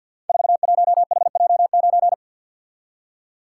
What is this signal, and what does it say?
Morse code, heard as a keyed tone, about 49 wpm, 700 Hz. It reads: 41HJ9
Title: Code Groups - 5 Letter:Numbers - 50wpm